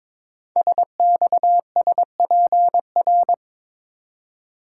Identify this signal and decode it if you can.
Morse code, heard as a keyed tone, about 22 wpm, 700 Hz. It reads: SXSPR
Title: Code Groups - 5 Letters - 22wpm